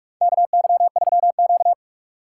Morse code, heard as a keyed tone, about 44 wpm, 700 Hz. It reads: KY3Q
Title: USA Call Signs - 4 Characters - 4x Character Spacing - 45wpm